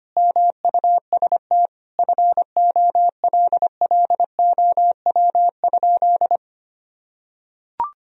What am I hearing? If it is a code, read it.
Morse code, heard as a keyed tone, about 25 wpm, 700 Hz. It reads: MUST FOLLOW?
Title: Sentences from Top 1000 Words - Review Part 2 - 25wpm